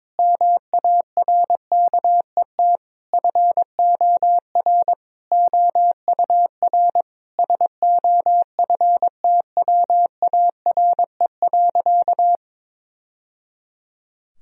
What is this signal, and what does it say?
Morse code, heard as a keyed tone, about 22 wpm, 700 Hz. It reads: MARKET FOR OUR SOFTWARE.
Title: Sentences from Top 1000 Words - Review Part 3 - 22wpm